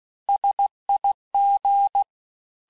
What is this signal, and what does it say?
Morse code, heard as a keyed tone, about 16 wpm, 800 Hz. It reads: SIG
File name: cours-du-lundi files / Cours-1586-texte-abreviations-vitesse-15mpm-puis-20mpm